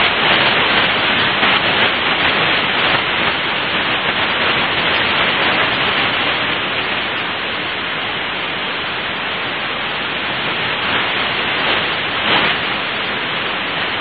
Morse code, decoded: none